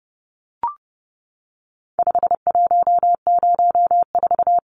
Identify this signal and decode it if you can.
Morse code, heard as a keyed tone, about 30 wpm, 700 Hz. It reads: 5104
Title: Numbers - 4-Digit 30wpm